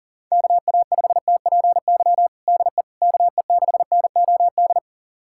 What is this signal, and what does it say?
Morse code, heard as a keyed tone, about 40 wpm, 700 Hz. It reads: KA5TPY DE KE6NOD
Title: USA Callsign DE Callsign - 6 Characters - 40wpm